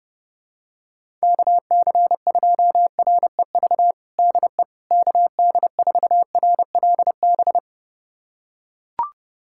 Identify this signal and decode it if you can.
Morse code, heard as a keyed tone, about 30 wpm, 700 Hz. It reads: KC2REV DE KD4RLB
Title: USA Callsign DE Callsign - 6 Characters - 30wpm